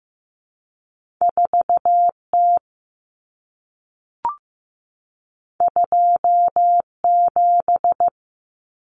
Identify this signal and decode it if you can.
Morse code, heard as a keyed tone, about 15 wpm, 700 Hz. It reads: 4T 27
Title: Common Cut Numbers - 2-Digit 15wpm